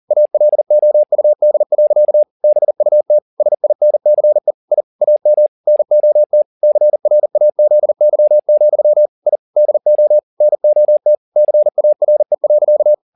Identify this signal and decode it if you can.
Morse code, heard as a keyed tone, about 40 wpm, 600 Hz. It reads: ALOUD. BUT SINCE I AM NOT CRAZY, I DO NOT CARE.